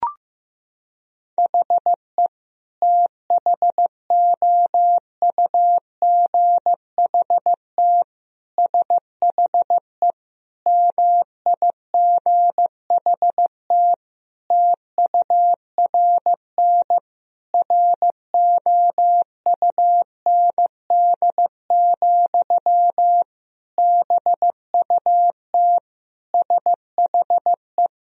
Morse code, heard as a keyed tone, about 15 wpm, 700 Hz. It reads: HE THOUGHT SHE MIGHT TURN ROUND, BUT SHE
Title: Sentences from Top 1000 Words - Review Part 1 - 15wpm